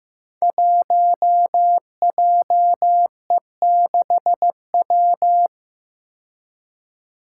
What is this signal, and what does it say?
Morse code, heard as a keyed tone, about 15 wpm, 700 Hz. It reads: 1JE6W
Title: Five Challenging Letter-Number Combinations 15wpm